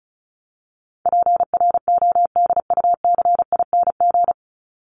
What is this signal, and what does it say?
Morse code, heard as a keyed tone, about 34 wpm, 700 Hz. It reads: PRODUCING
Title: ING Suffix - Top 100 Words with Base Words 35wpm